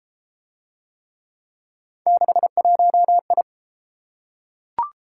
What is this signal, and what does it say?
Morse code, heard as a keyed tone, about 29 wpm, 700 Hz. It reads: 61I